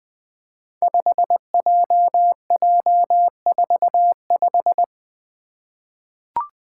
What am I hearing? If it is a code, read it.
Morse code, heard as a keyed tone, about 20 wpm, 700 Hz. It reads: 5JJ45
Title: Five Challenging Letter-Number Combinations 20wpm